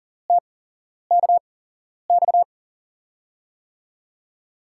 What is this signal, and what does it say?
Morse code, heard as a keyed tone, about 40 wpm, spaced as for 11 wpm, 700 Hz. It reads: TKX
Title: USA Call Signs - 5 Characters - 8x Character Spacing - 40wpm